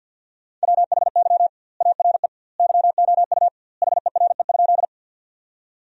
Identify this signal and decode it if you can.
Morse code, heard as a keyed tone, about 51 wpm, 700 Hz. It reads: WHY ARE YOU HERE?
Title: 2x Word-Spacing - Sentences from Top 200 Words - 50wpm